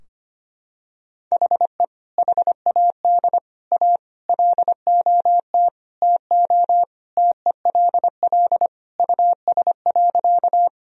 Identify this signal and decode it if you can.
Morse code, heard as a keyed tone, about 25 wpm, 700 Hz. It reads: HE HAD A LOT TO TELL US.
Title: Sentences from Top 1000 Words - Review Part 3 - 25wpm